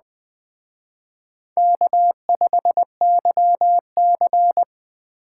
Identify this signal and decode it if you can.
Morse code, heard as a keyed tone, about 20 wpm, 700 Hz. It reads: K5YC
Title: Speed Racing - USA 1x2 Call Signs 17wpm